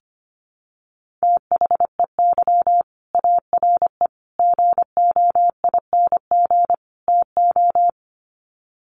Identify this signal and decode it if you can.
Morse code, heard as a keyed tone, about 25 wpm, 700 Hz. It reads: THEY ARE GOING TO
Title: Sets of 4 Words 25wpm